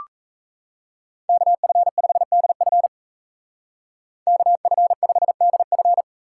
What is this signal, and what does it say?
Morse code, heard as a keyed tone, about 33 wpm, 700 Hz. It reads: KF5DF KF5DF
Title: Speed Racing - USA 2x2 Call Signs 28wpm